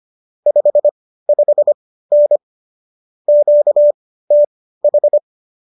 Morse code, heard as a keyed tone, about 25 wpm, spaced as for 15 wpm, 590 Hz. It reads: 55N QTH